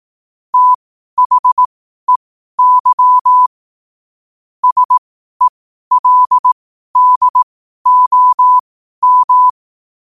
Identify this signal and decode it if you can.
Morse code, heard as a keyed tone, about 18 wpm, spaced as for 12 wpm, 1000 Hz. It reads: THEY SELDOM
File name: qotdfiles / 12wpm